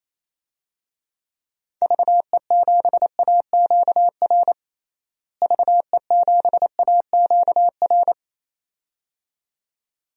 Morse code, heard as a keyed tone, about 28 wpm, 700 Hz. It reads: VE7AQR VE7AQR
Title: DXCC-100 Call Signs - 6 Characters - Repeated 2x - 28wpm